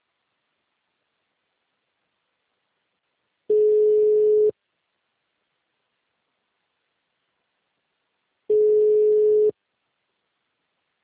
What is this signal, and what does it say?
Ringback tone (Europe, 425 Hz).